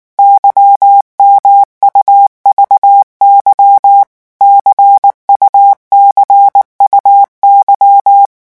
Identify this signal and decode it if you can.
Morse code, heard as a keyed tone, about 19 wpm, 800 Hz. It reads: YMUVY CUCUY
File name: morse mp3 / revision2-vitesse 18 mots